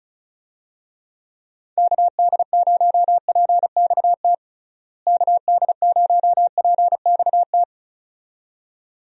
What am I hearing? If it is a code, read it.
Morse code, heard as a keyed tone, about 36 wpm, 700 Hz. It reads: KD0PXT KD0PXT
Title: USA Call Signs - 6 Characters - Repeated 2x - 35wpm